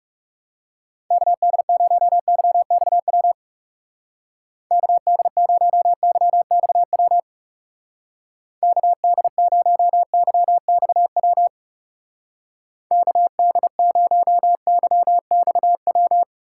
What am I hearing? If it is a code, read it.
Morse code, heard as a keyed tone, about 37 wpm, 700 Hz. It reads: KD0YXW KD0YXW KD0YXW KD0YXW
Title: Speed Racing - USA 2x3 Call Signs 30wpm